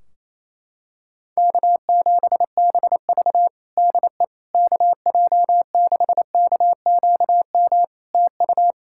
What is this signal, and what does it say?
Morse code, heard as a keyed tone, about 28 wpm, 700 Hz. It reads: K7BV DE KJ6KQM TU
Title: Practice QSO - Exchange 4 - Ending - 28wpm